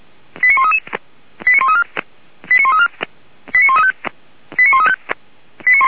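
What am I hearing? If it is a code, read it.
Five-tone selective call (ZVEI), heard as quick runs of five tones, tones of about 70 ms. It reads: 89133 89134 89134 89135 89135
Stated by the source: ZVEI five-tone selective call received by SDR.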